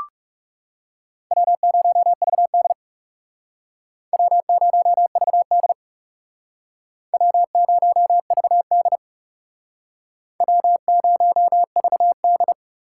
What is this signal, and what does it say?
Morse code, heard as a keyed tone, about 40 wpm, 700 Hz. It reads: W0VD W0VD W0VD W0VD